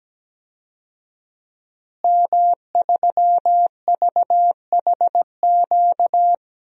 Morse code, heard as a keyed tone, about 17 wpm, 700 Hz. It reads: M3VHQ
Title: DXCC-100 Call Signs - 5 Characters - Repeated 2x - 17wpm